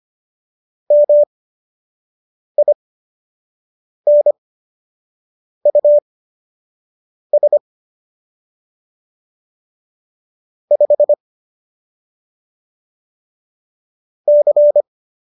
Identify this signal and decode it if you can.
Morse code, heard as a keyed tone, about 25 wpm, spaced as for 6 wpm, 590 Hz. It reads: MINUS 5 C
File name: qso012 06